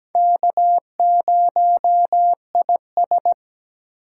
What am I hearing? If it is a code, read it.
Morse code, heard as a keyed tone, about 17 wpm, 700 Hz. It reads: K0IS
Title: USA Call Signs - 4x Character Spacing - 17wpm